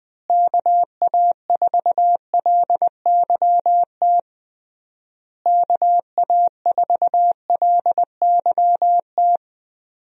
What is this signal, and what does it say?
Morse code, heard as a keyed tone, about 20 wpm, 700 Hz. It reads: KA4LYT KA4LYT
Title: USA Call Signs - 6 Characters - Repeated 3x - 20wpm